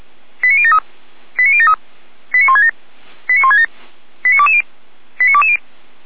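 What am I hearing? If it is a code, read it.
Five-tone selective call (ZVEI), heard as quick runs of five tones, tones of about 70 ms. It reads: 89073 89073 89167 89167 89220 89220
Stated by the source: ZVEI five-tone selective call 89163, received by SDR.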